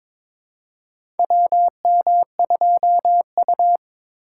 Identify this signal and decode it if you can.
Morse code, heard as a keyed tone, about 22 wpm, 700 Hz. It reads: WM2U
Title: USA Call Signs - 4 Characters - 3x Character Spacing - 22wpm